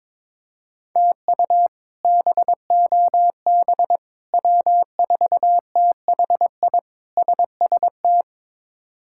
Morse code, heard as a keyed tone, about 22 wpm, 700 Hz. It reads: TU BOB W4THI SST
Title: K1USN SST QSO Practice - 22wpm